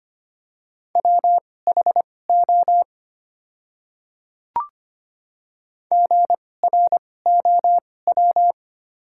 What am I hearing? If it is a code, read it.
Morse code, heard as a keyed tone, about 25 wpm, spaced as for 18 wpm, 700 Hz. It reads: WHO GROW
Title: 2x Character Spacing - Top 300 Words - 25wpm